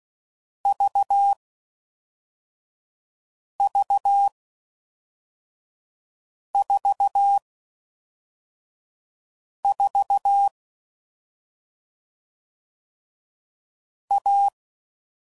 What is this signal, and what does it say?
Morse code, heard as a keyed tone, about 16 wpm, 800 Hz. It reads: V V 4 4 A
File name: Combinations / Difficult Combos / AUV AUV4